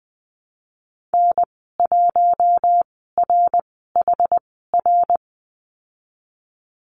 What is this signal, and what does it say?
Morse code, heard as a keyed tone, about 20 wpm, spaced as for 15 wpm, 700 Hz. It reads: N1RHR
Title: 147 Lesson 24g - Letter y - Call Signs 20wpm